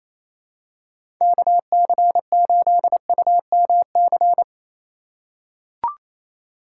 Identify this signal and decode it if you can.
Morse code, heard as a keyed tone, about 28 wpm, 700 Hz. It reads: KC8UMC
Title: USA Call Signs - 6 Characters - 3x Character Spacing - 28wpm